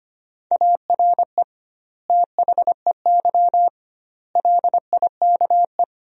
Morse code, heard as a keyed tone, about 25 wpm, 700 Hz. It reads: ARE THEY LIKE
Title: 2x Word-Spacing - Sentences from Top 200 Words - 25wpm